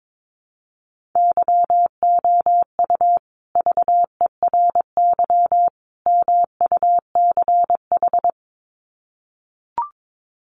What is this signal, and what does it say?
Morse code, heard as a keyed tone, about 22 wpm, 700 Hz. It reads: YOU VERY MUCH
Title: Sets of 3 Words 22wpm